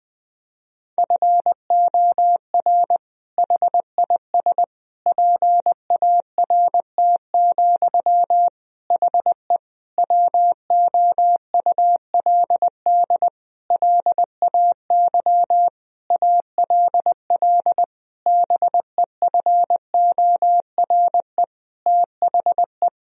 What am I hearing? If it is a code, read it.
Morse code, heard as a keyed tone, about 20 wpm, 700 Hz. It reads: FOR HIS PART, HE WOULD LAY ALL BEFORE THE
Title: Sentences from Top 1000 Words - Review Part 2 - 20wpm